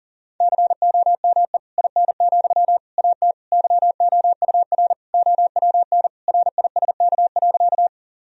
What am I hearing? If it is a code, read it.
Morse code, heard as a keyed tone, about 40 wpm, 700 Hz. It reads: COME IN, AT YOUR OWN RISK.
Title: Sentences from Top 1000 Words - Review Part 4 - 40wpm